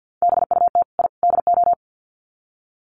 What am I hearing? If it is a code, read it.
Morse code, heard as a keyed tone, about 50 wpm, 700 Hz. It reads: BUT I DO
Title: Sets of 3 Words 50wpm